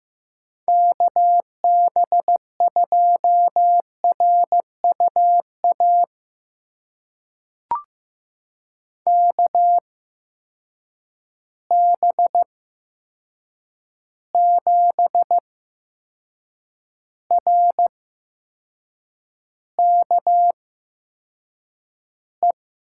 Morse code, heard as a keyed tone, about 15 wpm, spaced as for 6 wpm, 700 Hz. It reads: KB2RUA KB7RKE
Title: USA Call Signs - 6 Characters - 8x Character Spacing - 15wpm